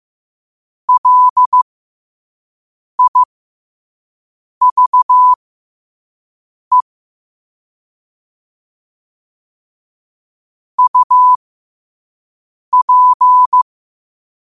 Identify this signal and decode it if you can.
Morse code, heard as a keyed tone, about 15 wpm, spaced as for 5 wpm, 1000 Hz. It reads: LIVE UP
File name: qotdfiles / 5wpm